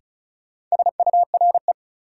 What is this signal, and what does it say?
Morse code, heard as a keyed tone, about 36 wpm, 700 Hz. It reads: SURE
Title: Sets of 3 Words - New Word Drill - 35wpm